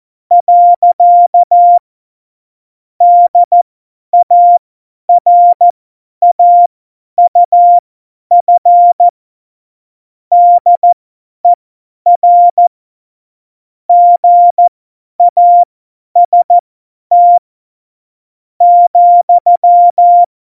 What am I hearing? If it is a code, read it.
Morse code, heard as a keyed tone, about 14 wpm, spaced as for 9 wpm, 700 Hz. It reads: . DARAUF DER GAST,
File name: Morsekurs / Witze / Witz 02. April 2019, 50 F70